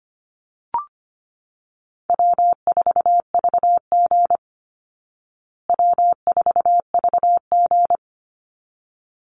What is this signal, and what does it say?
Morse code, heard as a keyed tone, about 25 wpm, 700 Hz. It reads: W4VG W4VG
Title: Easy Call Signs Repeated 2x - 25wpm